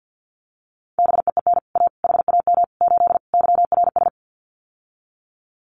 Morse code, heard as a keyed tone, about 50 wpm, 700 Hz. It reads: BEEN A HAM 8 YRS
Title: Common QSO Phrases Encore 50wpm